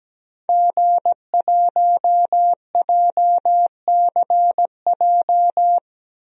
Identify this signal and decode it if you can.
Morse code, heard as a keyed tone, about 17 wpm, 700 Hz. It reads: G1JCJ
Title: DXCC-100 Call Signs - 5 Characters - Repeated 3x - 17wpm